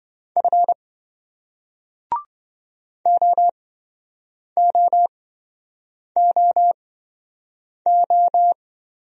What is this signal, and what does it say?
Morse code, heard as a keyed tone, about 26 wpm, spaced as for 7 wpm, 700 Hz. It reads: F OOOO